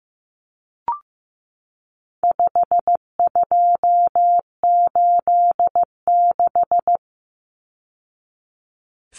Morse code, heard as a keyed tone, about 15 wpm, 700 Hz. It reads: 5286
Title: Numbers - 4-Digit 15wpm